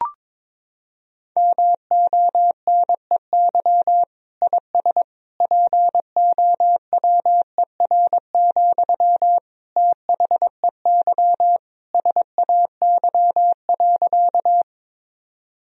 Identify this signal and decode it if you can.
Morse code, heard as a keyed tone, about 22 wpm, 700 Hz. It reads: MONEY IS POWER, THEY SAY.
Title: Sentences from Top 1000 Words - Review Part 2 - 22wpm